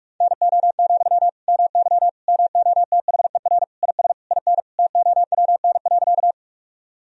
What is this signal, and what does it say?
Morse code, heard as a keyed tone, about 46 wpm, 700 Hz. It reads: NO, MY MOTHER IS IN TOWN.
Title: Sentences from Top 1000 Words - Review Part 2 - 45wpm